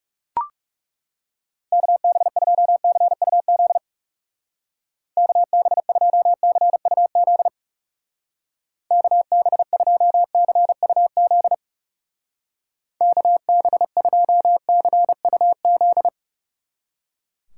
Morse code, heard as a keyed tone, about 36 wpm, 700 Hz. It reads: KB2CUZ KB2CUZ KB2CUZ KB2CUZ
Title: Speed Racing - USA 2x3 Call Signs 30wpm